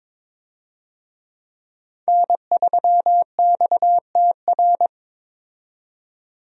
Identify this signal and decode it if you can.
Morse code, heard as a keyed tone, about 22 wpm, 700 Hz. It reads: N3XTR
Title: USA Call Signs - 5 Characters - 22wpm